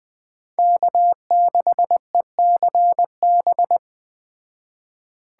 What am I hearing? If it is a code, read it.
Morse code, heard as a keyed tone, about 20 wpm, 700 Hz. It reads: K6ECB
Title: Speed Racing - USA 1x3 Call Signs 17wpm